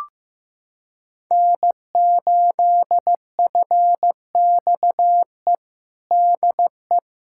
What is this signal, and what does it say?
Morse code, heard as a keyed tone, about 15 wpm, 700 Hz. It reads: N8FXE DE
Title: USA Callsign DE Callsign 15wpm